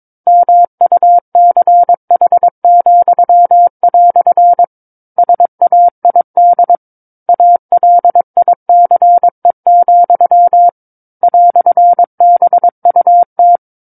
Morse code, heard as a keyed tone, about 22 wpm, 700 Hz. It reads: MUCH," SAID ALICE, "BUT
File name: 06-Chapter-06 22 WPM